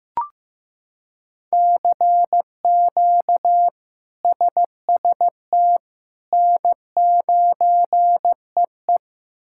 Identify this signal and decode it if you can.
Morse code, heard as a keyed tone, about 15 wpm, 700 Hz. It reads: CQ SST N9EE
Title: K1USN SST QSO Practice - 15wpm